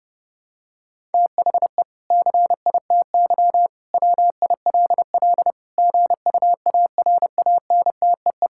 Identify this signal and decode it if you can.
Morse code, heard as a keyed tone, about 30 wpm, 700 Hz. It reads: THE CITY WILL GUARANTEE
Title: Sentences from Top 1600 Words 30wpm